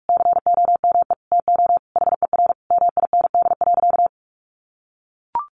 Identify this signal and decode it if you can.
Morse code, heard as a keyed tone, about 44 wpm, 700 Hz. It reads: COME TO HER MIND.
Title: Sentences from Top 1300 Words 45wpm